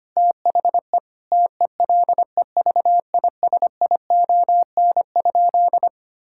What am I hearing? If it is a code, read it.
Morse code, heard as a keyed tone, about 25 wpm, 700 Hz. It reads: THE TELEVISION?
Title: Sentences from Top 1000 Words - Review Part 5 - 25wpm